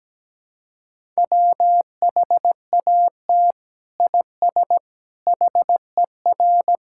Morse code, heard as a keyed tone, about 17 wpm, 700 Hz. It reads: WHAT IS HER
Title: Sentences from Top 1000 Words - Review Part 6 - 17wpm